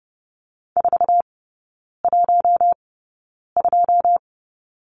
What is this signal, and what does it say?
Morse code, heard as a keyed tone, about 30 wpm, spaced as for 9 wpm, 700 Hz. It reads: 412
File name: Lesson 18e - Letter m - Three Characters 30wpm